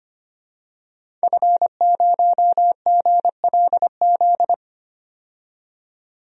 Morse code, heard as a keyed tone, about 25 wpm, 700 Hz. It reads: F0GLZ
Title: DXCC-100 Call Signs - 5 Characters - 25wpm